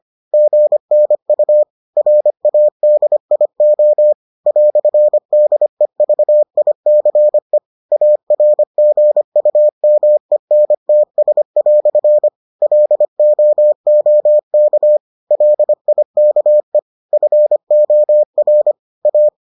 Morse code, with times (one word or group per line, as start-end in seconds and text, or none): 0.3-1.6 GNU
2.0-4.1 RADIO
4.5-7.6 "DEVICE
7.9-12.3 ARGUMENTS"
12.6-15.0 LOOK
15.3-16.8 LIKE
17.1-18.7 FOR
19.1-19.3 A